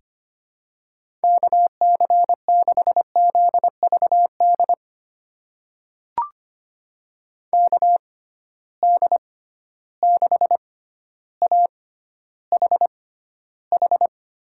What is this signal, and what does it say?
Morse code, heard as a keyed tone, about 25 wpm, spaced as for 13 wpm, 700 Hz. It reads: KC6ZVD KD6AHH
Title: USA Call Signs - 6 Characters - 6x Character Spacing - 25wpm